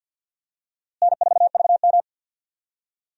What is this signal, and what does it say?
Morse code, heard as a keyed tone, about 51 wpm, 700 Hz. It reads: N4VM
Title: USA Call Signs - 4 Characters - Repeated 3x - 50wpm